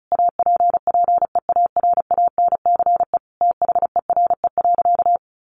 Morse code, heard as a keyed tone, about 34 wpm, 700 Hz. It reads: APPEARANCE THERE.
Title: Sentences from Top 1100 Words 35wpm